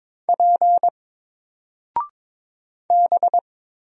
Morse code, heard as a keyed tone, about 22 wpm, 700 Hz. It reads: P B
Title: Single Letter-Number 22wpm